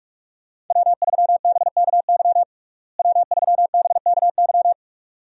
Morse code, heard as a keyed tone, about 45 wpm, 700 Hz. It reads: W3BKY W3BKY
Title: USA Call Signs - 5 Characters - Repeated 2x - 45wpm